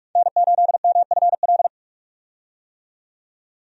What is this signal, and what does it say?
Morse code, heard as a keyed tone, about 42 wpm, 700 Hz. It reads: N8MFL
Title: USA Call Signs - 5 Characters - 45wpm